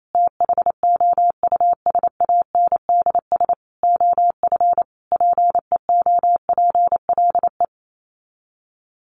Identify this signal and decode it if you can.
Morse code, heard as a keyed tone, about 28 wpm, 700 Hz. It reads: THOUSANDS OF PEOPLE
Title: Sets of 5 Words Encore2 28wpm